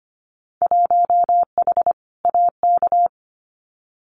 Morse code, heard as a keyed tone, about 25 wpm, 700 Hz. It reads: 1H AK
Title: Winter Field Day Exchanges Encore 25wpm